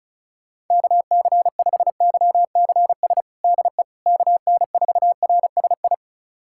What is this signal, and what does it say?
Morse code, heard as a keyed tone, about 36 wpm, 690 Hz. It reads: KC5YCS DE KN4RSI
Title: USA Callsign DE Callsign - 6 Characters - 35wpm